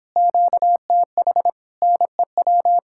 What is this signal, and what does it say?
Morse code, heard as a keyed tone, about 29 wpm, 700 Hz. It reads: QTH NEW